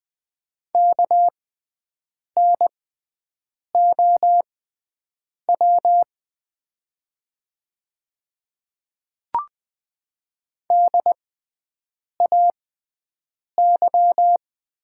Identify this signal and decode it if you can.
Morse code, heard as a keyed tone, about 20 wpm, spaced as for 7 wpm, 700 Hz. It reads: KNOW DAY
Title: 6x Character Spacing - Top 100 Words - 20wpm